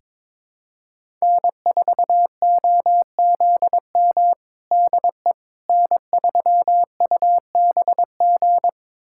Morse code, heard as a keyed tone, about 22 wpm, 700 Hz. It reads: N4OZM DE N3UBG
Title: USA Callsign DE Callsign - 5 Characters - 22wpm